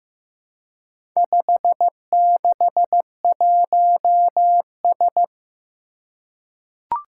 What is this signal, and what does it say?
Morse code, heard as a keyed tone, about 15 wpm, 700 Hz. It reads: 561S
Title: Four Challenging Letter-Number Combinations 15wpm